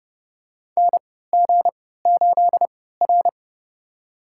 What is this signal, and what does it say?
Morse code, heard as a keyed tone, about 30 wpm, spaced as for 17 wpm, 700 Hz. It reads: NG8R
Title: USA Call Signs - 4 Characters - 3x Character Spacing - 30wpm